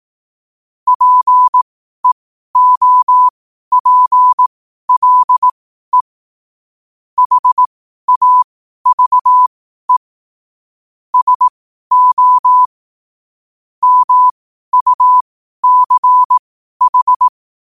Morse code, heard as a keyed tone, about 18 wpm, spaced as for 12 wpm, 1000 Hz. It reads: PEOPLE HAVE SO MUCH
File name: qotdfiles / 12wpm